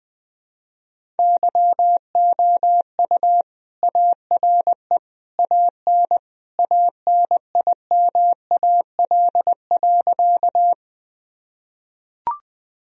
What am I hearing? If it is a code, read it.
Morse code, heard as a keyed tone, about 21 wpm, 700 Hz. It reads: YOU ARE AN ANIMAL.
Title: Sentences from Top 1000 Words - Review Part 1 - 20wpm